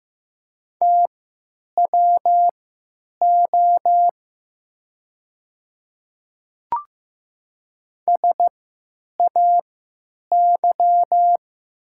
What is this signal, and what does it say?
Morse code, heard as a keyed tone, about 15 wpm, spaced as for 9 wpm, 700 Hz. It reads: TWO SAY
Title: 3x Character Spacing - Top 200 Words - 15wpm